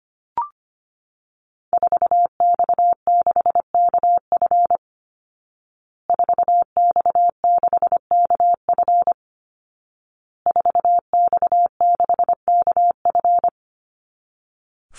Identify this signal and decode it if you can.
Morse code, heard as a keyed tone, about 25 wpm, 700 Hz. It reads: 4X6KF 4X6KF 4X6KF
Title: Call Signs Repeated 2x - 25wpm